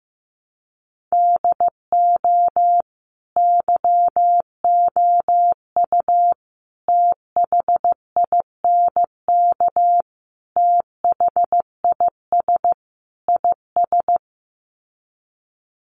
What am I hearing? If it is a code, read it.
Morse code, heard as a keyed tone, about 15 wpm, 700 Hz. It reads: DO YOU THINK THIS IS
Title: Sets of 5 Words Encore2 15wpm